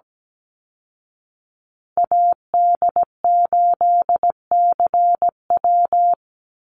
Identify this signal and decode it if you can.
Morse code, heard as a keyed tone, about 17 wpm, 700 Hz. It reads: AD8CW
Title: Call Signs Repeated 3x - 17wpm